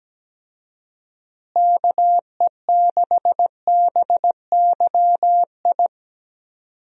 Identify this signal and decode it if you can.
Morse code, heard as a keyed tone, about 17 wpm, 700 Hz. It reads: KE6BYI
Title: USA Call Signs - 6 Characters - Repeated 3x - 17wpm